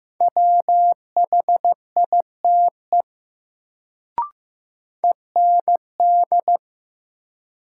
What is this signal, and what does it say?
Morse code, heard as a keyed tone, about 15 wpm, 700 Hz. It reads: WHITE END
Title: Sets of 2 Words - New Word Drill - 15wpm